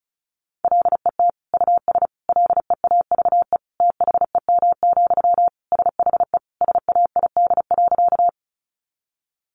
Morse code, heard as a keyed tone, about 34 wpm, 700 Hz. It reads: LET US LEAVE THEM, SHE SAID.
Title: Sentences from Top 400 Words 35wpm